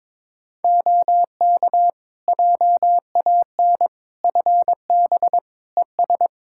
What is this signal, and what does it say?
Morse code, heard as a keyed tone, about 22 wpm, 700 Hz. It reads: OK JAN FB ES
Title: K1USN SST QSO Practice 22wpm